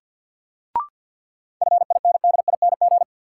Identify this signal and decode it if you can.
Morse code, heard as a keyed tone, about 50 wpm, 700 Hz. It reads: FINDING
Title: Sentences from Top 1100 Words - New Word Drill - 50wpm